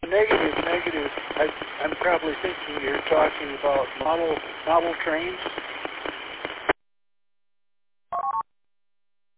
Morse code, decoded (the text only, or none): none